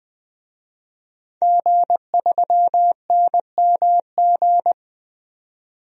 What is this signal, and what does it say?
Morse code, heard as a keyed tone, about 20 wpm, 700 Hz. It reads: G3NMG
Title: DXCC-100 Call Signs - 5 Characters - 20wpm